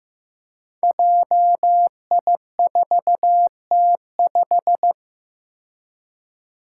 Morse code, heard as a keyed tone, about 15 wpm, 700 Hz. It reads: JI4T5
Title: Five Challenging Letter-Number Combinations 15wpm